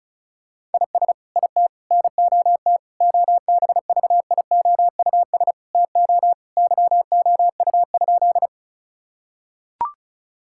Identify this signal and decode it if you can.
Morse code, heard as a keyed tone, about 36 wpm, 700 Hz. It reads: IS IT NOT OBVIOUS TO YOU?
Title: Sentences from Top 1600 Words 35wpm